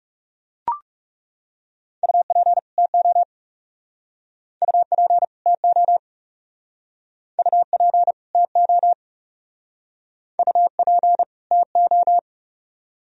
Morse code, heard as a keyed tone, about 35 wpm, 700 Hz. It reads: UP TO UP TO UP TO UP TO
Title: Speed-Racing - Sets of 2 Words 30wpm